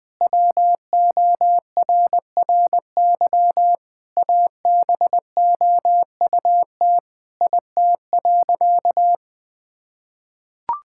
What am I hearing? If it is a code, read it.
Morse code, heard as a keyed tone, about 20 wpm, 700 Hz. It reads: WORRY ABOUT IT.
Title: Sentences with 6-Word Phrases - Spoken Beginning - 20wpm